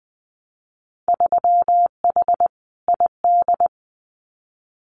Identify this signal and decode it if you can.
Morse code, heard as a keyed tone, about 20 wpm, 700 Hz. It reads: 3H ID
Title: Winter Field Day Exchanges Encore 20wpm